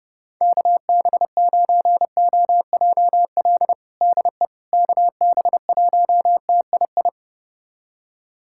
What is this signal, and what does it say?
Morse code, heard as a keyed tone, about 32 wpm, 700 Hz. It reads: KB9OJL DE KB1TII
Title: USA Callsign DE Callsign 30wpm